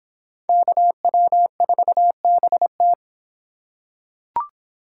Morse code, heard as a keyed tone, about 24 wpm, 700 Hz. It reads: KW4BT